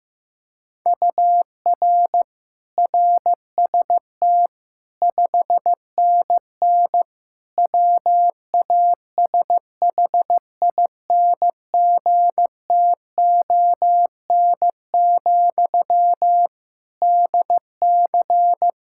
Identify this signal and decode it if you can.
Morse code, heard as a keyed tone, about 15 wpm, 700 Hz. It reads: UR RST 5NN WASHINGTON, DC
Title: Practice QSO - Exchange 1 - Beginning - 15wpm